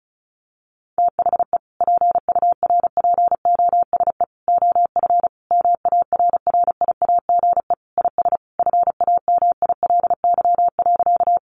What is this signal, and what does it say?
Morse code, heard as a keyed tone, about 34 wpm, 700 Hz. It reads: THE PURPOSE OF MARRIAGE IS FAMILY.
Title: Sentences from Top 800 Words 35wpm